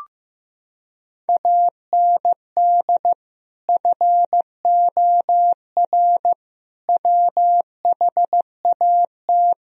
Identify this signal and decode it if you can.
Morse code, heard as a keyed tone, about 14 wpm, 700 Hz. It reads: AND FOR WHAT
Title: Sentences from Top 1000 Words - Review Part 1 - 15wpm